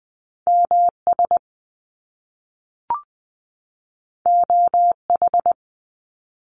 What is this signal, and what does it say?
Morse code, heard as a keyed tone, about 20 wpm, 700 Hz. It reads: MS OH
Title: US States 20wpm